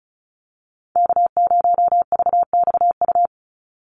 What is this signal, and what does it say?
Morse code, heard as a keyed tone, about 34 wpm, 700 Hz. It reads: K0VXU
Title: Call Signs Repeated 3x - 35wpm